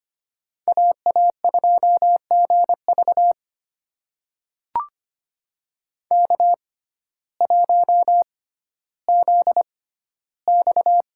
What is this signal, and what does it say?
Morse code, heard as a keyed tone, about 25 wpm, spaced as for 14 wpm, 700 Hz. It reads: AA2GV K1ZX
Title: USA Call Signs - 5 Characters - 6x Character Spacing - 25wpm